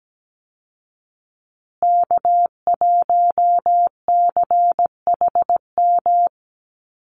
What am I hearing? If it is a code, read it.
Morse code, heard as a keyed tone, about 17 wpm, 700 Hz. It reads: K1CHM
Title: Call Signs Repeated 2x - 17wpm